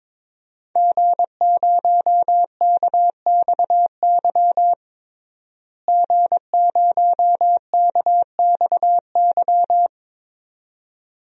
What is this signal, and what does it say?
Morse code, heard as a keyed tone, about 22 wpm, 700 Hz. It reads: G0KXY G0KXY
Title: DXCC-100 Call Signs - 5 Characters - Repeated 2x - 22wpm